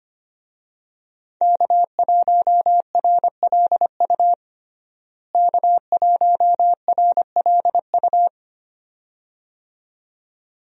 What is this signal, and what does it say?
Morse code, heard as a keyed tone, about 25 wpm, 700 Hz. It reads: K1RLU K1RLU
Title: USA Call Signs - 5 Characters - Repeated 2x - 25wpm